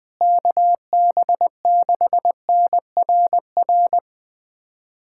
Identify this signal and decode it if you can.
Morse code, heard as a keyed tone, about 20 wpm, 700 Hz. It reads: KB6NRR
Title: USA Callsign DE Callsign 20wpm